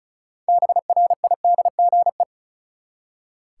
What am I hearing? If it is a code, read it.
Morse code, heard as a keyed tone, about 36 wpm, 700 Hz. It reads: BRIDGE
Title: Six Letter Words - 35wpm